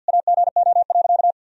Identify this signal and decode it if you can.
Morse code, heard as a keyed tone, about 46 wpm, 700 Hz. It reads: AGO.
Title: Sentences from Top 1000 Words - Review Part 1 - 50wpm